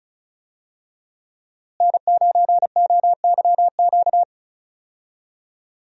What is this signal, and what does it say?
Morse code, heard as a keyed tone, about 34 wpm, 700 Hz. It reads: N9OYQ
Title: USA Call Signs - 5 Characters - Repeated 3x - 35wpm